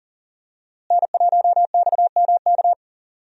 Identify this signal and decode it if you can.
Morse code, heard as a keyed tone, about 40 wpm, 700 Hz. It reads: N1XMK
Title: USA Call Signs - 5 Characters - Repeated 2x - 40wpm